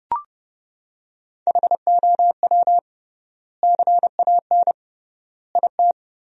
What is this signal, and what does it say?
Morse code, heard as a keyed tone, about 30 wpm, 700 Hz. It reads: HOW CAN IT
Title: 3x Word-Spacing - Sentences from Top 100 Words - 30wpm